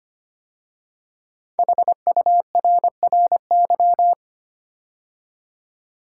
Morse code, heard as a keyed tone, about 25 wpm, 700 Hz. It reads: HURRY
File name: 5 Letter Words - First 3 Letters Given - 25wpm